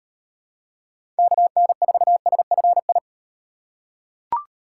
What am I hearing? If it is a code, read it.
Morse code, heard as a keyed tone, about 38 wpm, 700 Hz. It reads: KN4SFI